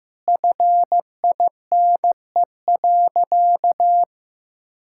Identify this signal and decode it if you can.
Morse code, heard as a keyed tone, about 16 wpm, 700 Hz. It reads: FINE.
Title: Sentences from Top 1000 Words - Review Part 5 - 15wpm